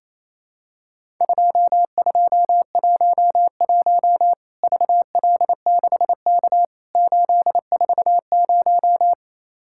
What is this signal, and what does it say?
Morse code, heard as a keyed tone, about 28 wpm, 700 Hz. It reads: 2211 VL6K 840
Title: Worked All Europe DX Contest - QTC Exchange - 28wpm